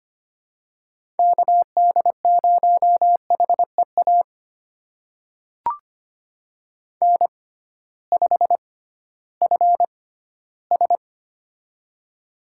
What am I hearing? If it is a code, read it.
Morse code, heard as a keyed tone, about 25 wpm, spaced as for 15 wpm, 700 Hz. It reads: KD0HEA N5FS
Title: USA Call Signs - 6x Character Spacing - 25wpm